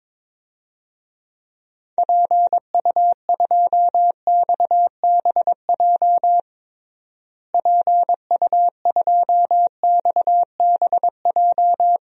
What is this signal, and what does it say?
Morse code, heard as a keyed tone, about 22 wpm, 700 Hz. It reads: PU2XBJ PU2XBJ
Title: DXCC-100 Call Signs - 6 Characters - Repeated 2x - 22wpm